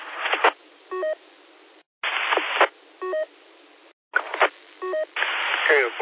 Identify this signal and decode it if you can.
Speech. There is no code.